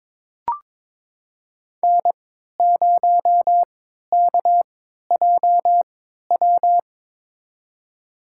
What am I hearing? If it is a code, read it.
Morse code, heard as a keyed tone, about 22 wpm, spaced as for 12 wpm, 700 Hz. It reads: N0KJW
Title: USA Call Signs - 5 Characters - 3x Character Spacing - 22wpm